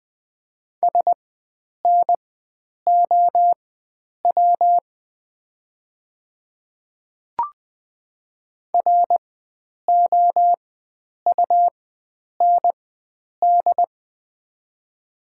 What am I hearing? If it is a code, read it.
Morse code, heard as a keyed tone, about 20 wpm, spaced as for 9 wpm, 690 Hz. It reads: SNOW ROUND
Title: 4x Character Spacing - Top 500 Words - 20wpm